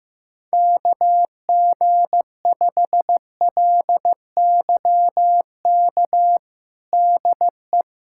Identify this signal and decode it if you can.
Morse code, heard as a keyed tone, about 15 wpm, 700 Hz. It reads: KG5LYK DE
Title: USA Callsign DE Callsign - 6 Characters - 15wpm